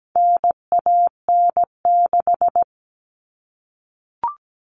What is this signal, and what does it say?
Morse code, heard as a keyed tone, about 17 wpm, 700 Hz. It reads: NAN6
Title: All Cut Numbers - 4-Digit 17wpm